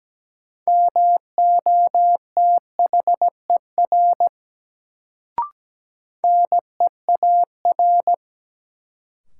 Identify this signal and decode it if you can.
Morse code, heard as a keyed tone, about 17 wpm, 700 Hz. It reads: MOTHER NEAR
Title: Sentences from Top 200 Words - New Word Drill - 17wpm